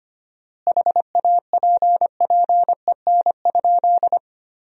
Morse code, heard as a keyed tone, about 25 wpm, 700 Hz. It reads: HAPPEN?
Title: 4x Word-Spacing - Sentences from Top 400 Words - 25wpm